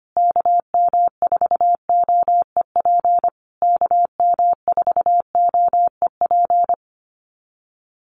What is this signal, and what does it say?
Morse code, heard as a keyed tone, about 25 wpm, 700 Hz. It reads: KM4OEP KM4OEP
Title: USA Call Signs Repeated 3x 25wpm